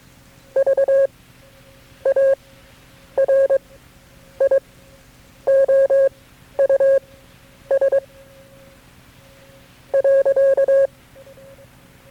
Morse code, heard as a keyed tone, about 22 wpm, spaced as for 9 wpm, 540 Hz. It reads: VARIOUS .
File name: FreeFiles / CW-practice / KB6MT-tape3-side1